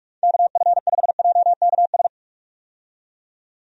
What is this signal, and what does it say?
Morse code, heard as a keyed tone, about 47 wpm, 700 Hz. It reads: KF5JKS
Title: USA Call Signs - 6 Characters - 45wpm